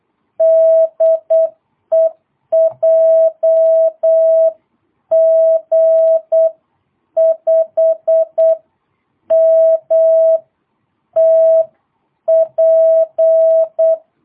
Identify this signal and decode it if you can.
Morse code, heard as a keyed tone, about 8 wpm, spaced as for 7 wpm, 650 Hz. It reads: DE JG5MTP
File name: jh8cbh / 2022cw